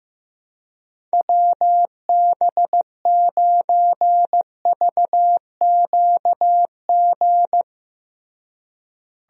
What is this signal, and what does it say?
Morse code, heard as a keyed tone, about 15 wpm, 700 Hz. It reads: WB9VQG
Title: USA Call Signs - 6 Characters - Repeated 2x - 15wpm